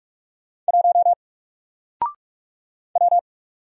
Morse code, heard as a keyed tone, about 45 wpm, 700 Hz. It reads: 1 W
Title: Single Letter-Number Alt 45wpm